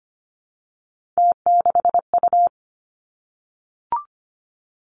Morse code, heard as a keyed tone, about 25 wpm, 700 Hz. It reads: T6U
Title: Three Challenging Letter-Number Combinations 25wpm